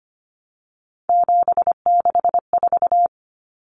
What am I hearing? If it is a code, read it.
Morse code, heard as a keyed tone, about 25 wpm, 690 Hz. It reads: 764
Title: Numbers - 3-Digit 25wpm